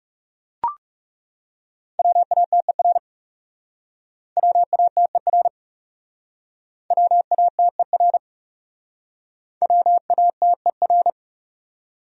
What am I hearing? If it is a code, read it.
Morse code, heard as a keyed tone, about 39 wpm, 700 Hz. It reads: WATER WATER WATER WATER